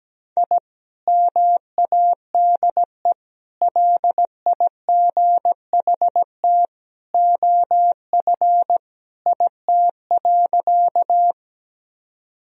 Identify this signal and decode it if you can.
Morse code, heard as a keyed tone, about 17 wpm, 700 Hz. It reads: I MADE LIGHT OF IT.
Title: Sentences from Top 1000 Words - Review Part 1 - 17wpm